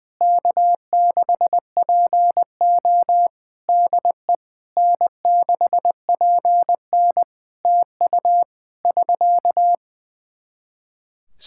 Morse code, heard as a keyed tone, about 21 wpm, 700 Hz. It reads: K6PO DE N6PN TU <SK>
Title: K1USN SST QSO Practice 20wpm